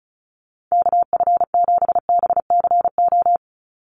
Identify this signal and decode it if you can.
Morse code, heard as a keyed tone, about 36 wpm, 700 Hz. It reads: KF7BCO
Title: USA Call Signs 35wpm